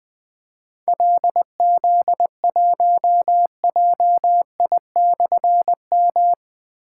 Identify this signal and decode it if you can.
Morse code, heard as a keyed tone, about 20 wpm, 700 Hz. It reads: LZ1JI/M
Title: Difficult Call Signs 20wpm